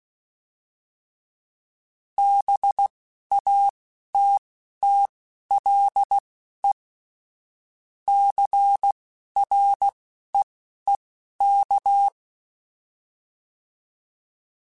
Morse code, heard as a keyed tone, about 16 wpm, spaced as for 11 wpm, 800 Hz. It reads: BATTLE CREEK
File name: fromSMs / FromSMs mp3s / Cities@10